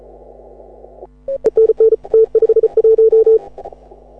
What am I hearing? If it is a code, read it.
Morse code, heard as a keyed tone, about 36 wpm, 440 Hz. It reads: E NN T51
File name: AUDIO2018 / HA6OI CW